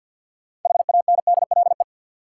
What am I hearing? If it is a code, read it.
Morse code, heard as a keyed tone, about 48 wpm, 700 Hz. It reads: HANDLE
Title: Sentences from Top 1100 Words - New Word Drill Alt - 50wpm